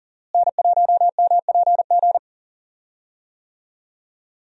Morse code, heard as a keyed tone, about 40 wpm, 700 Hz. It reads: N1MPG
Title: USA Call Signs - 5 Characters - Repeated 2x - 40wpm